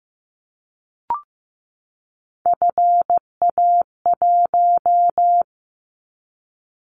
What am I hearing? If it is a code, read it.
Morse code, heard as a keyed tone, about 15 wpm, 700 Hz. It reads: FA1
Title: Callsign Prefixes for Top 25 DXCC - 15wpm